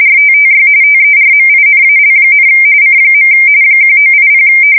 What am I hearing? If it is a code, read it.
RTTY, about 45.45 baud. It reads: RTTY TERMINAL, C. 1980
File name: fsk170 / teletype-RTTY